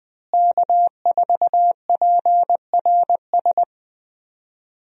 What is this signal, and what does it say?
Morse code, heard as a keyed tone, about 20 wpm, 700 Hz. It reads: K4PRS
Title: USA Call Signs - 5 Characters - Repeated 3x - 20wpm